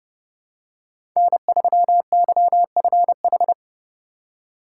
Morse code, heard as a keyed tone, about 30 wpm, 700 Hz. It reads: N3YFH
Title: USA Call Signs - 5 Characters - 4x Character Spacing - 30wpm